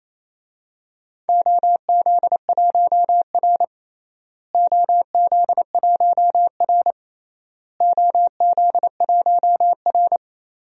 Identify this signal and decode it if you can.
Morse code, heard as a keyed tone, about 27 wpm, 700 Hz. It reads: OZ1R OZ1R OZ1R
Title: DXCC-100 Call Signs - 4 Characters - Repeated 3x - 28wpm